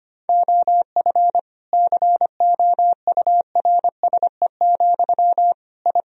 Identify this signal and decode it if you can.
Morse code, heard as a keyed tone, about 25 wpm, 700 Hz. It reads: OF COURSE, I
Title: Sentences from Top 1000 Words - Review Part 4 - 25wpm